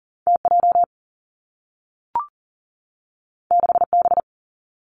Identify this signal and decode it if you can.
Morse code, heard as a keyed tone, about 41 wpm, 700 Hz. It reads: TJ 6B
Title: Two Challenging Letter-Number Combinations 40wpm